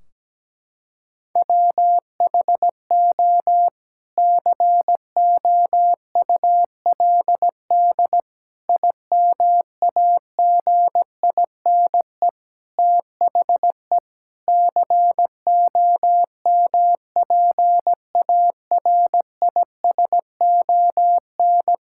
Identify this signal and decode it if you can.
Morse code, heard as a keyed tone, about 17 wpm, 700 Hz. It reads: WHO COULD IMAGINE THE COMPARISON
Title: Sentences from Top 1600 Words 17wpm